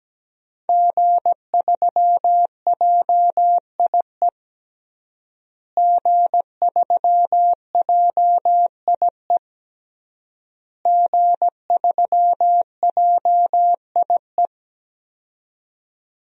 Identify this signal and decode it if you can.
Morse code, heard as a keyed tone, about 17 wpm, 700 Hz. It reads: G3JIE G3JIE G3JIE
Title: DXCC-100 Call Signs - 5 Characters - Repeated 3x - 17wpm